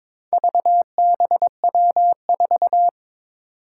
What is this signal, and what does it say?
Morse code, heard as a keyed tone, about 22 wpm, 700 Hz. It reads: VBW4
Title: Four Challenging Letter-Number Combinations 22wpm